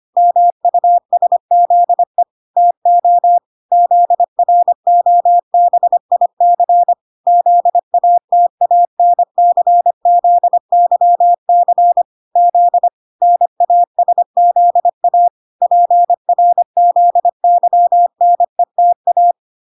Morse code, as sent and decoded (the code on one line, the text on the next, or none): -- ..- ... --.. / - --- / --.. .-. --- -... .. -.-. / --.. .- - .- -. -.-. --.. -.-- -.-. / --.. / -. .- ... --.. .- / .--. .-. --.. -.-- -. - .-
MUSZE TO ZROBIC ZATANCZYC Z NASZA PRZYNETA